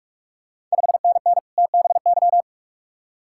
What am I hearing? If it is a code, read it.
Morse code, heard as a keyed tone, about 46 wpm, 700 Hz. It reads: 5NN TBY
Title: Ontario QSO Party Exchange 45wpm